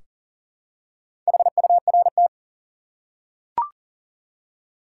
Morse code, heard as a keyed tone, about 40 wpm, 700 Hz. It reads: HURT
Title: Four Letter Words - 40wpm